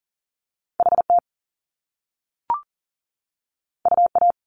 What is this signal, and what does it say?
Morse code, heard as a keyed tone, about 40 wpm, 700 Hz. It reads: HT UA
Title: Top Two Letter Combinations in Words 40wpm